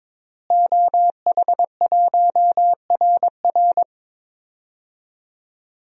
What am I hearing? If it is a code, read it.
Morse code, heard as a keyed tone, about 22 wpm, 700 Hz. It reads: OH1RR
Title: DXCC-100 Call Signs - 5 Characters - Repeated 3x - 22wpm